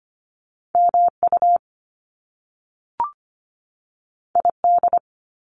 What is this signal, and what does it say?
Morse code, heard as a keyed tone, about 25 wpm, 700 Hz. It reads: MU ID
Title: Top Two Letter Combinations in Words 25wpm